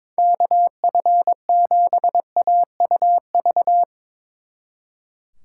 Morse code, heard as a keyed tone, about 24 wpm, 700 Hz. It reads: KF7AUV
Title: Speed Racing - USA 2x3 Call Signs 22wpm